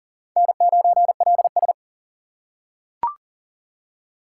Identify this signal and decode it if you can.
Morse code, heard as a keyed tone, about 40 wpm, 700 Hz. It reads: N9LS
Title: USA Call Signs - 4 Characters - 3x Character Spacing - 40wpm